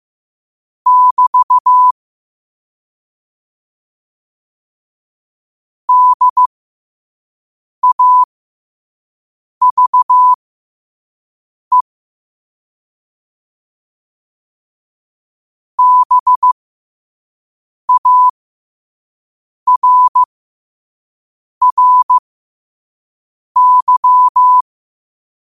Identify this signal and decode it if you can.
Morse code, heard as a keyed tone, about 15 wpm, spaced as for 5 wpm, 1000 Hz. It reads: <BT> DAVE BARRY
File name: qotdfiles / 5wpm